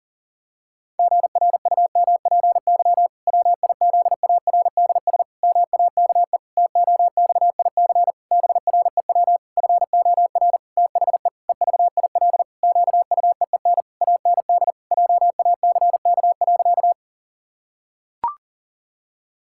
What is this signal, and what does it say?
Morse code, heard as a keyed tone, about 40 wpm, 700 Hz. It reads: GRUMPY WIZARDS MAKE TOXIC BREW FOR THE EVIL QUEEN AND JACK.
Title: Pangram Challenge v2 - 40wpm